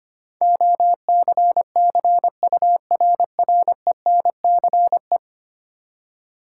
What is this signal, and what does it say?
Morse code, heard as a keyed tone, about 25 wpm, 700 Hz. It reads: OCCURRENCE
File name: Ten Letter Words - 25wpm